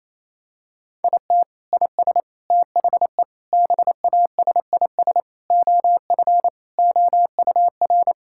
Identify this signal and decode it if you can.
Morse code, heard as a keyed tone, about 27 wpm, 700 Hz. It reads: IT IS THE BASIS OF OUR
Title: Sentences from Top 1000 Words - Review Part 5 - 28wpm